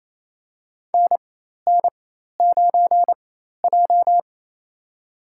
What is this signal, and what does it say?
Morse code, heard as a keyed tone, about 28 wpm, spaced as for 13 wpm, 700 Hz. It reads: NN9J
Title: USA Call Signs - 4 Characters - 4x Character Spacing - 28wpm